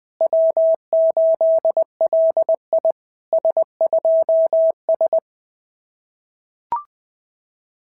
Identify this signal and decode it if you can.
Morse code, heard as a keyed tone, about 20 wpm, 660 Hz. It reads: W8LI S2S
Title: SOTA QSO Practice 20wpm